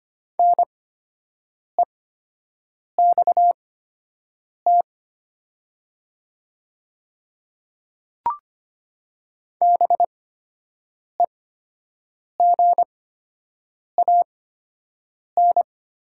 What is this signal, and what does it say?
Morse code, heard as a keyed tone, about 25 wpm, spaced as for 7 wpm, 700 Hz. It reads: NEXT BEGAN
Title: 8x Character Spacing - Top 300 Words - 25wpm